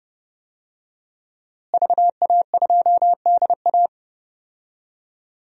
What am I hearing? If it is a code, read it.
Morse code, heard as a keyed tone, about 30 wpm, 700 Hz. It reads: VA2DA
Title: DXCC-100 Call Signs - 5 Characters - Repeated 3x - 30wpm